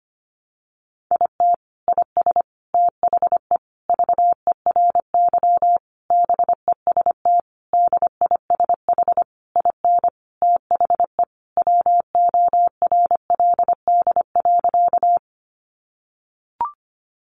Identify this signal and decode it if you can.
Morse code, heard as a keyed tone, about 25 wpm, 700 Hz. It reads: IT IS THE VERY BEST DISH IN THE WORLD.
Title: Sentences from Top 1300 Words 25wpm